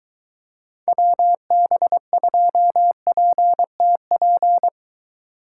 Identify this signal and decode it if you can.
Morse code, heard as a keyed tone, about 23 wpm, 700 Hz. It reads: WB2PTP